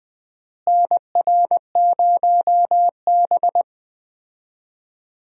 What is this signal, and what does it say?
Morse code, heard as a keyed tone, about 20 wpm, 700 Hz. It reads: NR0B
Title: USA Call Signs - 4 Characters - Repeated 3x - 20wpm